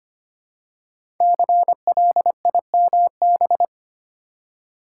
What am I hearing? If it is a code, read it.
Morse code, heard as a keyed tone, about 25 wpm, 700 Hz. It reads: CLIMB
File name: 5 Letter Words - First 3 Letters Given - 25wpm